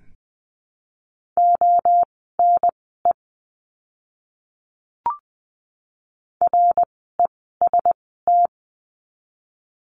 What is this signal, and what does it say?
Morse code, heard as a keyed tone, about 19 wpm, spaced as for 14 wpm, 700 Hz. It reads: ONE REST
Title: 068 Lesson 12f - Letter d - Words 20wpm